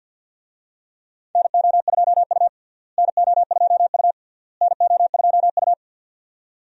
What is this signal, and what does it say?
Morse code, heard as a keyed tone, about 50 wpm, 700 Hz. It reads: NO2U NO2U NO2U
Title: USA Call Signs - 4 Characters - Repeated 3x - 50wpm